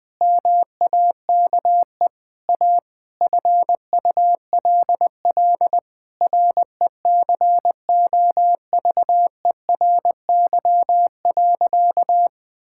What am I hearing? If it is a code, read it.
Morse code, heard as a keyed tone, about 20 wpm, 700 Hz. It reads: MAKE A FULL RECOVERY.
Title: Sentences with 7-Word Phrases - 20wpm